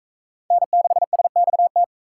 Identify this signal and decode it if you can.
Morse code, heard as a keyed tone, about 37 wpm, 700 Hz. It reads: N6SXT